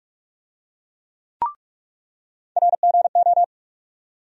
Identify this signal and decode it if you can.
Morse code, heard as a keyed tone, about 41 wpm, 700 Hz. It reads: RGO
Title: USA Call Sign Postfixes - 3 Letters 45wpm